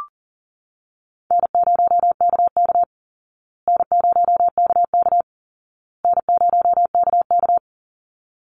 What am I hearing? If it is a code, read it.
Morse code, heard as a keyed tone, about 40 wpm, 700 Hz. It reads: N0KK N0KK N0KK
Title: Easy Call Signs Repeated 3x - 40wpm